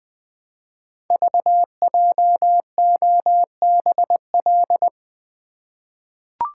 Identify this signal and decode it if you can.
Morse code, heard as a keyed tone, about 20 wpm, 700 Hz. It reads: VJOBL
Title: Code Groups - 5 Letters - 20wpm